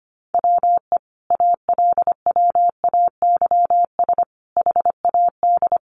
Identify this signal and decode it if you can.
Morse code, heard as a keyed tone, about 25 wpm, 700 Hz. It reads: WE ALWAYS HAD
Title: Sentences from Top 1300 Words 25wpm